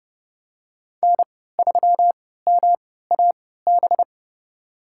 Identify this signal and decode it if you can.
Morse code, heard as a keyed tone, about 30 wpm, spaced as for 17 wpm, 700 Hz. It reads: N3MAB
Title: USA Call Signs - 5 Characters - 3x Character Spacing - 30wpm